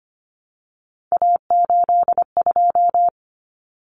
Morse code, heard as a keyed tone, about 25 wpm, 700 Hz. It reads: A82
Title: Common Cut Numbers - 3-Digit 25wpm